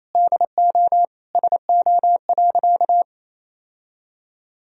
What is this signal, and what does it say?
Morse code, heard as a keyed tone, about 28 wpm, 700 Hz. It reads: DO SO.
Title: Sentences from Top 1000 Words - Review Part 1 - 28wpm